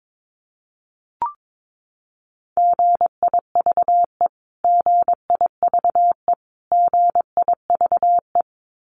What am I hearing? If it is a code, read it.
Morse code, heard as a keyed tone, about 22 wpm, 700 Hz. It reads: GIVE GIVE GIVE
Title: Top 500 Words Repeated 22wpm